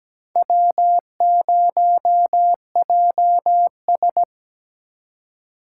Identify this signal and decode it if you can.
Morse code, heard as a keyed tone, about 20 wpm, 700 Hz. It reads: W0JS
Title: Speed Racing - USA 1x2 Call Signs 17wpm